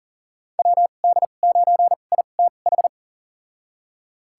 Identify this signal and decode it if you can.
Morse code, heard as a keyed tone, about 37 wpm, spaced as for 28 wpm, 700 Hz. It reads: WD9ITH
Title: USA Call Signs - 6 Characters - 2x Character Spacing - 40wpm